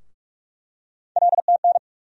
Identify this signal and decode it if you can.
Morse code, heard as a keyed tone, about 42 wpm, 700 Hz. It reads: LTN
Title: USA Call Sign Postfixes - 3 Letters 45wpm